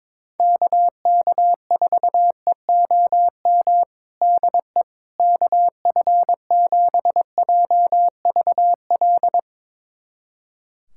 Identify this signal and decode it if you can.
Morse code, heard as a keyed tone, about 22 wpm, 700 Hz. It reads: KK4EOM DE KF7JVL
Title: USA Callsign DE Callsign - 6 Characters - 22wpm